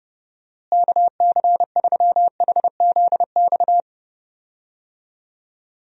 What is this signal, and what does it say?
Morse code, heard as a keyed tone, about 32 wpm, 700 Hz. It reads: KC3HZX
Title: USA Call Signs - 6 Characters - Repeated 3x - 30wpm